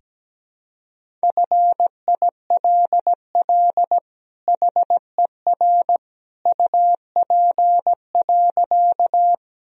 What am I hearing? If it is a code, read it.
Morse code, heard as a keyed tone, about 17 wpm, 700 Hz. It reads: FILL HER UP.
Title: Sentences from Top 1000 Words - Review Part 3 - 17wpm